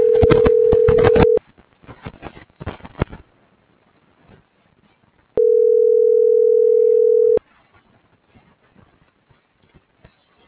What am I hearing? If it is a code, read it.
Ringback tone (North America), 440 + 480 Hz.